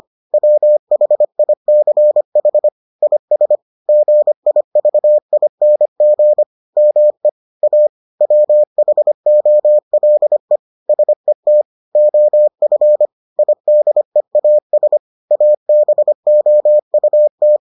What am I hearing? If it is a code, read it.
Morse code, heard as a keyed tone, about 25 wpm, 600 Hz. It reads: WHICH IS GIVING ME A WHOLE SET OF IDEAS ABOUT